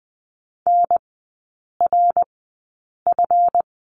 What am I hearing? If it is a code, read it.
Morse code, heard as a keyed tone, about 20 wpm, spaced as for 8 wpm, 690 Hz. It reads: NRF
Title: 145 Lesson 24e - Letter y - Three Characters 20wpm